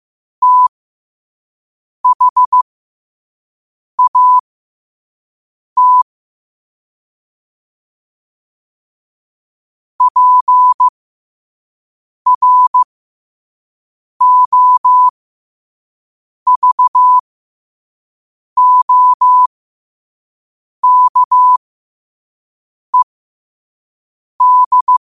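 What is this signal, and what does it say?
Morse code, heard as a keyed tone, about 15 wpm, spaced as for 5 wpm, 1000 Hz. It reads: THAT PROVOKED
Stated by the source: Morse practice at 5 wpm.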